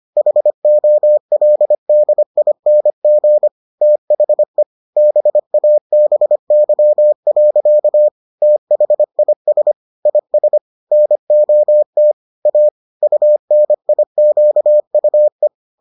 Morse code, heard as a keyed tone, about 25 wpm, 600 Hz. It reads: HOLDING THE BABY. THIS IS NOT A UNIQUE